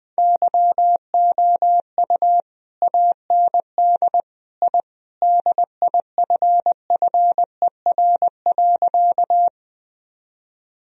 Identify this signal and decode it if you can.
Morse code, heard as a keyed tone, about 20 wpm, 700 Hz. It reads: YOU AND I DIFFER.
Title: Sentences from Top 1000 Words - Review Part 1 - 20wpm